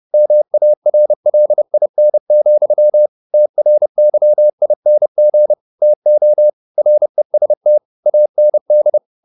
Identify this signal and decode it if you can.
Morse code, heard as a keyed tone, about 30 wpm, 600 Hz. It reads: MARLIN, TRYING TO REST AND